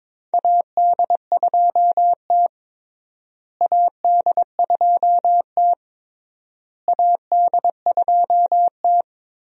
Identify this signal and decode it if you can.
Morse code, heard as a keyed tone, about 23 wpm, 700 Hz. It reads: AD2T AD2T AD2T
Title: USA Call Signs - 4 Characters - Repeated 3x - 22wpm